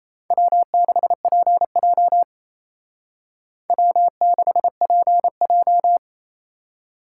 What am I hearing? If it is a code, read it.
Morse code, heard as a keyed tone, about 35 wpm, 700 Hz. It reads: W6PJ W6PJ